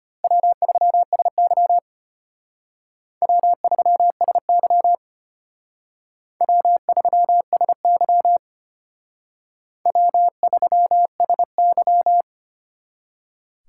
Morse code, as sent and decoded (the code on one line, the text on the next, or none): .-- ...-- ... -.-- / .-- ...-- ... -.-- / .-- ...-- ... -.-- / .-- ...-- ... -.--
W3SY W3SY W3SY W3SY